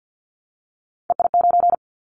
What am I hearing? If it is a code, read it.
Morse code, heard as a keyed tone, about 50 wpm, 700 Hz. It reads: EI9
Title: Callsign Prefixes for Top 50 DXCC - 50wpm